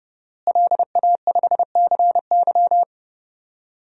Morse code, heard as a keyed tone, about 30 wpm, 700 Hz. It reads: LA5CY
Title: DXCC-100 Call Signs - 5 Characters - Repeated 3x - 30wpm